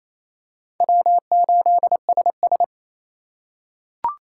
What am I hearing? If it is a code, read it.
Morse code, heard as a keyed tone, about 28 wpm, 700 Hz. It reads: W8SS
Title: USA Call Signs - 4 Characters - 8x Character Spacing - 28wpm